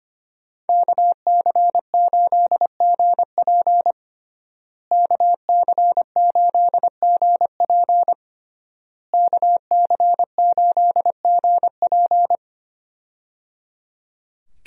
Morse code, heard as a keyed tone, about 25 wpm, 700 Hz. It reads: KC8GP KC8GP KC8GP
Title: USA Call Signs - 5 Characters - Repeated 3x - 25wpm